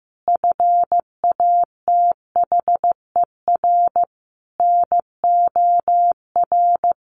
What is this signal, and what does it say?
Morse code, heard as a keyed tone, about 15 wpm, 700 Hz. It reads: FATHER NOR
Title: Sentences from Top 1500 Words 15wpm